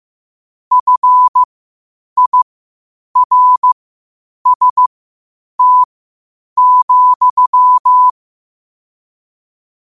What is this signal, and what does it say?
Morse code, heard as a keyed tone, about 15 wpm, spaced as for 8 wpm, 1000 Hz. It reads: FIRST,
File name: qotdfiles / 8wpm